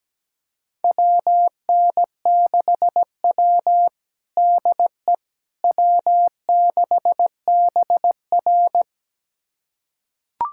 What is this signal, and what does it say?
Morse code, heard as a keyed tone, about 17 wpm, 700 Hz. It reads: WN6W DE W6BR
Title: USA Callsign DE Callsign - 4 Characters - 17wpm